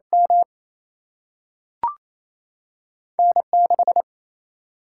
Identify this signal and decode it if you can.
Morse code, heard as a keyed tone, about 28 wpm, 700 Hz. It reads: M N 6
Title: Single-Letter Single-Number 28wpm